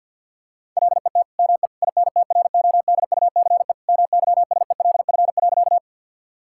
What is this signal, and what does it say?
Morse code, heard as a keyed tone, about 50 wpm, 700 Hz. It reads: LET ME INTRODUCE MYSELF.
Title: Sentences from Top 1600 Words 50wpm